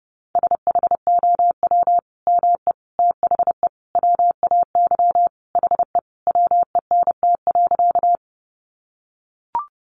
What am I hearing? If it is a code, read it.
Morse code, heard as a keyed tone, about 30 wpm, 700 Hz. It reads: SHOW ME THE WAY HE WENT.
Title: Sentences from Top 200 Words 30wpm